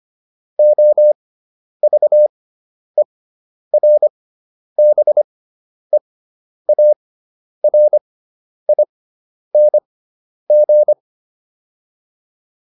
Morse code, heard as a keyed tone, about 25 wpm, spaced as for 10 wpm, 600 Hz. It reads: OVERBEARING